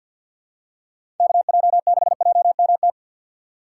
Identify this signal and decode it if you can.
Morse code, heard as a keyed tone, about 52 wpm, 700 Hz. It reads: KJ6JMT
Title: USA Call Signs - 6 Characters - 4x Character Spacing - 50wpm